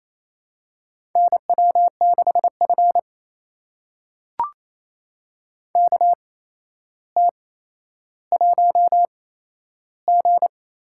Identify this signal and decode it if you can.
Morse code, heard as a keyed tone, about 28 wpm, spaced as for 12 wpm, 700 Hz. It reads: NW6F KT1G
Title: USA Call Signs - 4 Characters - 8x Character Spacing - 28wpm